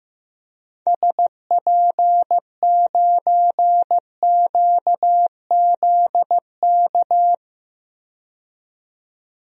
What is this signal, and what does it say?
Morse code, heard as a keyed tone, about 15 wpm, 700 Hz. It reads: SP9QZK
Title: DXCC-100 Call Signs - 6 Characters - 15wpm